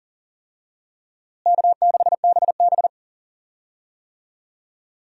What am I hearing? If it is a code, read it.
Morse code, heard as a keyed tone, about 37 wpm, 700 Hz. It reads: K6BB
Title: USA Call Signs - 4 Characters - Repeated 3x - 40wpm